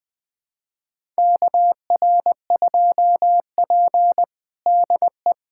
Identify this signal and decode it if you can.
Morse code, heard as a keyed tone, about 20 wpm, 700 Hz. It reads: KR2P DE
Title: USA Callsign DE Callsign - 4 Characters - 20wpm